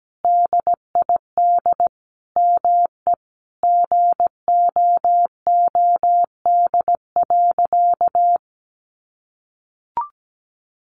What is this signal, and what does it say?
Morse code, heard as a keyed tone, about 17 wpm, 700 Hz. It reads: DID ME GOOD.
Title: Sentences from Top 1300 Words 17wpm